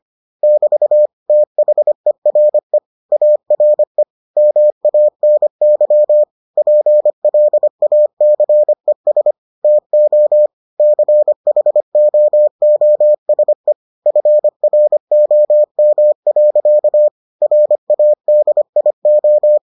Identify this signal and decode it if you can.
Morse code, heard as a keyed tone, about 25 wpm, 600 Hz. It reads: <BT> THERE ARE MANY PLACES TO CHOOSE FROM. RADIO